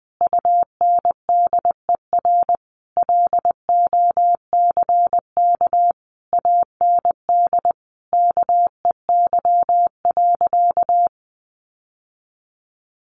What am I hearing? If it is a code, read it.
Morse code, heard as a keyed tone, about 20 wpm, 700 Hz. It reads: UNDER LOCK AND KEY.
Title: Sentences from Top 1400 Words 20wpm